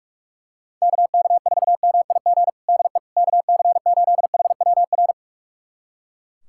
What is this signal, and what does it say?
Morse code, heard as a keyed tone, about 44 wpm, 700 Hz. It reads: KK4MIG DE KC8HWR
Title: USA Callsign DE Callsign 45wpm